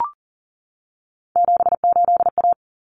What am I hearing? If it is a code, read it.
Morse code, heard as a keyed tone, about 40 wpm, 700 Hz. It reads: 78A
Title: Common Cut Numbers - 3-Digit 40wpm